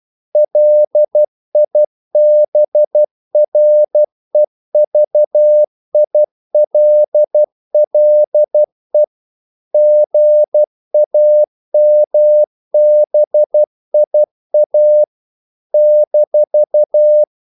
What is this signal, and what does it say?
Morse code, heard as a keyed tone, about 12 wpm, 600 Hz. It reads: LIBREVILLE GAMBIA -
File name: CW / Practice / NationalCapitals12